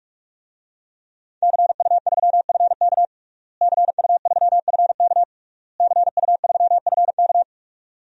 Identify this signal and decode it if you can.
Morse code, heard as a keyed tone, about 46 wpm, 700 Hz. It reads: CU3FK CU3FK CU3FK
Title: DXCC-100 Call Signs - 5 Characters - Repeated 3x - 45wpm